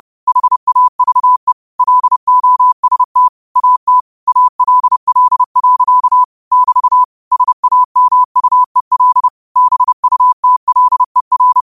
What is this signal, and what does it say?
Morse code, heard as a keyed tone, about 30 wpm, 1000 Hz. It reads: HAVE LOST AT ALL. <BT> SAMUEL BUTLER
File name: qotdfiles / 30wpm